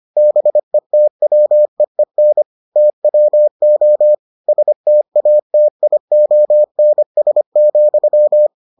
Morse code, heard as a keyed tone, about 25 wpm, 600 Hz. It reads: BETWEEN TWO STATIONS,